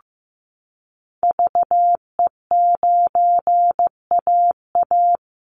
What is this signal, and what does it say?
Morse code, heard as a keyed tone, about 15 wpm, 700 Hz. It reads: VE9AA
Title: Call Signs Repeated 3x - 15wpm